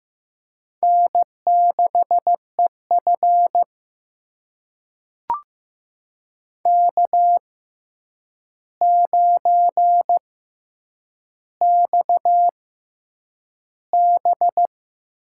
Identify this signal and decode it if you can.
Morse code, heard as a keyed tone, about 15 wpm, spaced as for 8 wpm, 700 Hz. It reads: N6EF K9XB
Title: USA Call Signs - 4 Characters - 6x Character Spacing - 15wpm